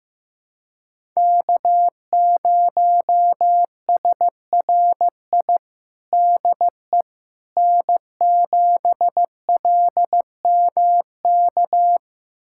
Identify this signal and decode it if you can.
Morse code, heard as a keyed tone, about 15 wpm, 700 Hz. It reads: K0SRI DE N7LMK
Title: USA Callsign DE Callsign - 5 Characters - 15wpm